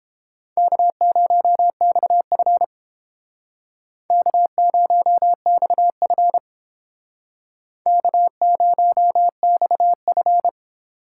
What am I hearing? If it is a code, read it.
Morse code, heard as a keyed tone, about 31 wpm, 700 Hz. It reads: K0XF K0XF K0XF